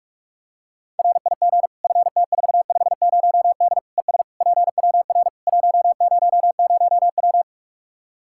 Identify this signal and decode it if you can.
Morse code, heard as a keyed tone, about 46 wpm, 700 Hz. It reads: RIG FT450D ES PWR 100W
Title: Practice QSO - Exchange 2 - Info - 45wpm